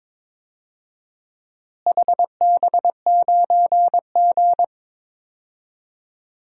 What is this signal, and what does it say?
Morse code, heard as a keyed tone, about 22 wpm, 700 Hz. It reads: HB9G
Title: DXCC-100 Call Signs - 4 Characters - Repeated 3x - 22wpm